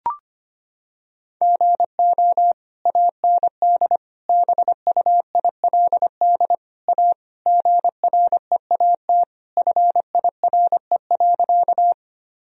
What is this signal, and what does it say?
Morse code, heard as a keyed tone, about 25 wpm, 700 Hz. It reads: GO AND BUILD A GREAT FIRE.
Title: Sentences from Top 1000 Words - Review Part 1 - 25wpm